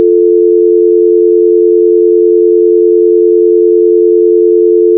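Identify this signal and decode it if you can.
Dial tone (North America), 350 + 440 Hz.
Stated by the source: Dial tone sound effect from the game Fear The Spotlight, gamerip.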